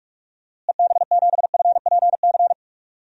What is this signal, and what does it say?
Morse code, heard as a keyed tone, about 46 wpm, 700 Hz. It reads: EB7FPC
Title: DXCC-100 Call Signs - 6 Characters - Repeated 3x - 45wpm